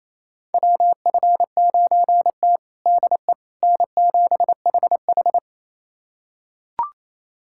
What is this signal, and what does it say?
Morse code, heard as a keyed tone, about 28 wpm, 700 Hz. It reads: WF9T DE N7HH
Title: USA Callsign DE Callsign - 4 Characters - 28wpm